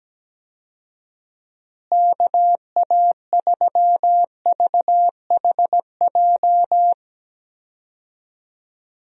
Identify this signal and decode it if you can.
Morse code, heard as a keyed tone, about 17 wpm, 700 Hz. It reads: KA3VHJ
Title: USA Call Signs - 6 Characters - Repeated 2x - 17wpm